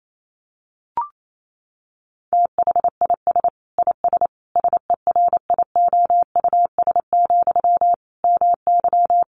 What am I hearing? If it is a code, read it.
Morse code, heard as a keyed tone, about 28 wpm, 700 Hz. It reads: THIS IS SERIOUS, MY
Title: Sentences from Top 1100 Words 28wpm